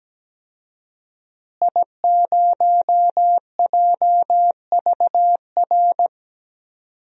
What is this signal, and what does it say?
Morse code, heard as a keyed tone, about 17 wpm, 700 Hz. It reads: I0JVR
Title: DXCC-100 Call Signs - 5 Characters - Repeated 3x - 17wpm